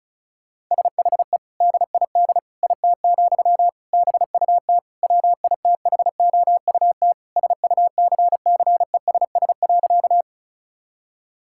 Morse code, heard as a keyed tone, about 34 wpm, 700 Hz. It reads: SHE DID IT, BUT WITHOUT SUCCESS.
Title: Sentences from Top 1000 Words - Review Part 6 - 35wpm